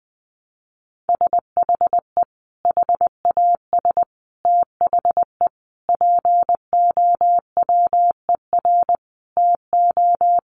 Morse code, heard as a keyed tone, about 20 wpm, 700 Hz. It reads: SHE HAS THE POWER TO
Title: Sentences from Top 1200 Words 20wpm